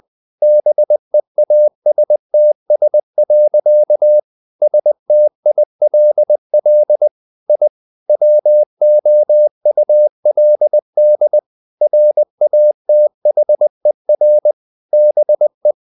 Morse code, heard as a keyed tone, about 20 wpm, 600 Hz. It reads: BEASTS. STILL I WOULD RATHER BE
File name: the old man and the sea / the old man and the sea part 1 20wpm